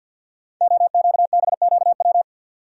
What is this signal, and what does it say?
Morse code, heard as a keyed tone, about 52 wpm, 700 Hz. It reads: OQBQW
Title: Code Groups - 5 Letters - 50wpm